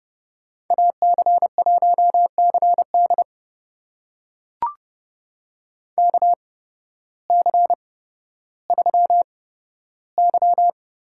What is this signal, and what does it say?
Morse code, heard as a keyed tone, about 31 wpm, spaced as for 14 wpm, 700 Hz. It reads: AC1CD KC3Y
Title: USA Call Signs - 5 Characters - 8x Character Spacing - 30wpm